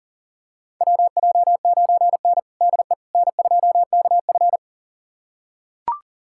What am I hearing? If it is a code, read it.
Morse code, heard as a keyed tone, about 40 wpm, 700 Hz. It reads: WJ9N DE N2KF
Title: USA Callsign DE Callsign - 4 Characters - 40wpm